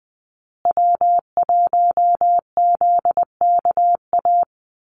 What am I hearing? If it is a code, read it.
Morse code, heard as a keyed tone, about 20 wpm, 700 Hz. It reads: W1ZKA
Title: USA Call Signs Repeated 3x 20wpm